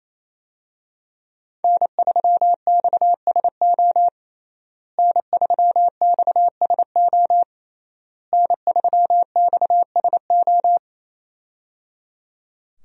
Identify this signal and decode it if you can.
Morse code, heard as a keyed tone, about 27 wpm, 700 Hz. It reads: N3XSO N3XSO N3XSO
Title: USA Call Signs - 5 Characters - Repeated 3x - 28wpm